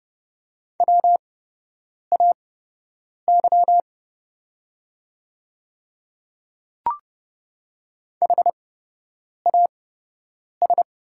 Morse code, heard as a keyed tone, about 30 wpm, spaced as for 8 wpm, 700 Hz. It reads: WAY HAS
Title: 8x Character Spacing - Top 100 Words - 30wpm